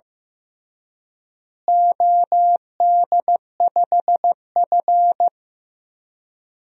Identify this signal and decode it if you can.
Morse code, heard as a keyed tone, about 15 wpm, 700 Hz. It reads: OD5F
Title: DXCC-100 Call Signs - 4 Characters - Repeated 3x - 15wpm